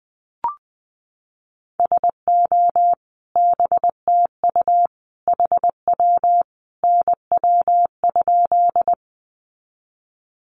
Morse code, heard as a keyed tone, about 20 wpm, 700 Hz. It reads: SO BTU HW NW?
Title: Common QSO Phrases 20wpm